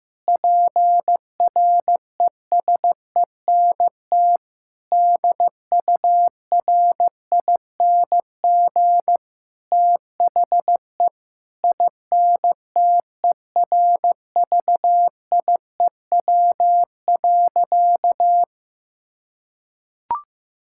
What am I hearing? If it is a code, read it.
Morse code, heard as a keyed tone, about 15 wpm, 700 Hz. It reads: PRESENT DURING THE INTERVIEW.
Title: Sentences from Top 1600 Words 15wpm